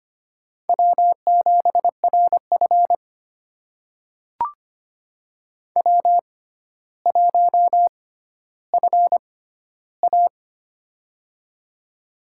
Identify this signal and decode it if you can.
Morse code, heard as a keyed tone, about 25 wpm, spaced as for 13 wpm, 700 Hz. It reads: W7RF W1FA
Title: USA Call Signs - 4 Characters - 6x Character Spacing - 25wpm